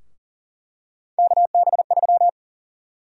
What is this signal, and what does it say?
Morse code, heard as a keyed tone, about 40 wpm, 700 Hz. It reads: KB3
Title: USA Call Sign Prefixes - Repeated 2x - 40wpm